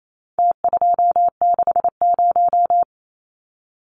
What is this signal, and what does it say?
Morse code, heard as a keyed tone, about 28 wpm, 700 Hz. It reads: T260
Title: Numbers - Contest Style - 4-Digit 28wpm